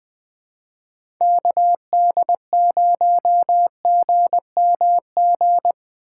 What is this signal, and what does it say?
Morse code, heard as a keyed tone, about 20 wpm, 700 Hz. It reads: KD0GMG
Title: USA Call Signs - 6 Characters - 4x Character Spacing - 20wpm